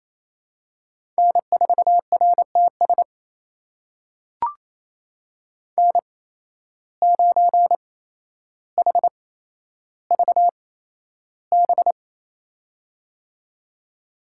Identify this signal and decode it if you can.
Morse code, heard as a keyed tone, about 28 wpm, spaced as for 12 wpm, 700 Hz. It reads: N4RTS N9HVB
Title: USA Call Signs - 5 Characters - 8x Character Spacing - 28wpm